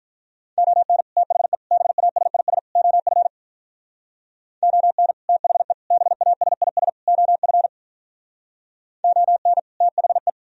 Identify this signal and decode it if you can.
Morse code, heard as a keyed tone, about 45 wpm, 700 Hz. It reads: ON THE BASIS OF ON THE BASIS OF ON THE